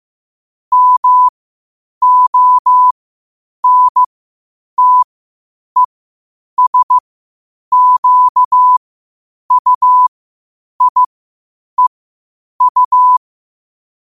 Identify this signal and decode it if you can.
Morse code, heard as a keyed tone, about 15 wpm, spaced as for 8 wpm, 1000 Hz. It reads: MONTESQUIEU
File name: qotdfiles / 8wpm